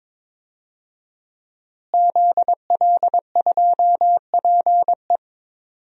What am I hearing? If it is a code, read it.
Morse code, heard as a keyed tone, about 22 wpm, 700 Hz. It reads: ZL2PE
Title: DXCC-100 Call Signs - 5 Characters - Repeated 3x - 22wpm